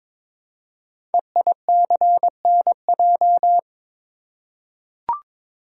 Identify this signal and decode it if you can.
Morse code, heard as a keyed tone, about 22 wpm, 700 Hz. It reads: EICNJ
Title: Code Groups - 5 Letters - 22wpm